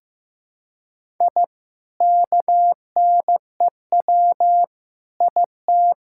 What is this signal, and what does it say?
Morse code, heard as a keyed tone, about 15 wpm, 700 Hz. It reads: I KNEW IT
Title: Sentences from Top 1000 Words - Review Part 5 - 15wpm